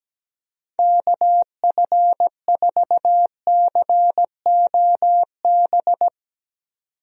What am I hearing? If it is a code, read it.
Morse code, heard as a keyed tone, about 17 wpm, 700 Hz. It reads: KF4COB
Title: USA Call Signs - 6 Characters - 17wpm